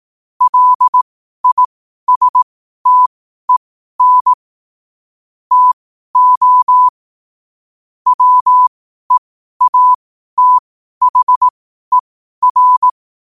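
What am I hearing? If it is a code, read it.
Morse code, heard as a keyed tone, about 18 wpm, spaced as for 12 wpm, 1000 Hz. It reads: LISTEN TO WEATHER
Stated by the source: Morse code quote of the day at 12 wpm.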